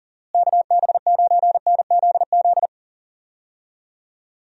Morse code, heard as a keyed tone, about 40 wpm, 700 Hz. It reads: KB9NZZ
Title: USA Call Signs - 6 Characters - Repeated 2x - 40wpm